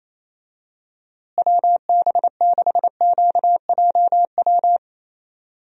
Morse code, heard as a keyed tone, about 27 wpm, 700 Hz. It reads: WB6QJW
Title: USA Call Signs - 6 Characters - Repeated 3x - 28wpm